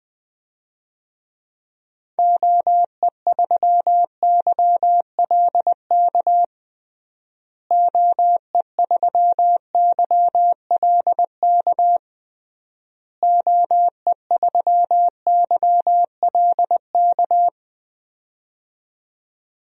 Morse code, heard as a keyed tone, about 21 wpm, 700 Hz. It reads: OE3YLK OE3YLK OE3YLK
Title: DXCC-100 Call Signs - 6 Characters - Repeated 3x - 20wpm